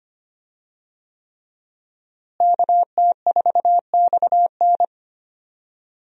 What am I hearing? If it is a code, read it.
Morse code, heard as a keyed tone, about 25 wpm, 700 Hz. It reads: KT4XN
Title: USA Call Signs - 5 Characters - Repeated 3x - 25wpm